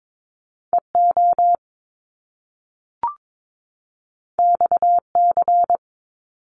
Morse code, heard as a keyed tone, about 22 wpm, 700 Hz. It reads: EO XC
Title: Top Two Letter Combinations in Words 22wpm